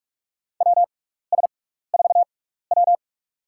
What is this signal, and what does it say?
Morse code, heard as a keyed tone, about 43 wpm, spaced as for 15 wpm, 700 Hz. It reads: WS4W
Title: USA Call Signs - 6x Character Spacing - 45wpm